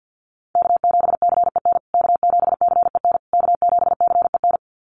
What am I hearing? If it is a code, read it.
Morse code, heard as a keyed tone, about 50 wpm, 700 Hz. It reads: K7CEN K7CEN K7CEN
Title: USA Call Signs Repeated 3x 50wpm